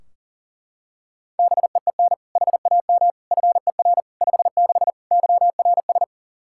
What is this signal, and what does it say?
Morse code, heard as a keyed tone, about 37 wpm, 700 Hz. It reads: BEEN HAM FER 56 YRS
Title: Practice QSO - Exchange 3 - More Info - 40wpm